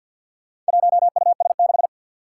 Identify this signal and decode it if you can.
Morse code, heard as a keyed tone, about 52 wpm, 700 Hz. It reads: 1US6
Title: Four Challenging Letter-Number Combinations 50wpm